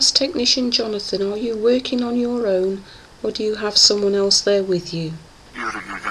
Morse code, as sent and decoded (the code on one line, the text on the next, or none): none